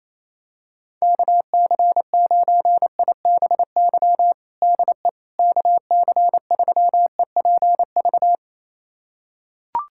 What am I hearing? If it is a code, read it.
Morse code, heard as a keyed tone, about 28 wpm, 700 Hz. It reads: KC9IBY DE KC3EPV
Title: USA Callsign DE Callsign - 6 Characters - 28wpm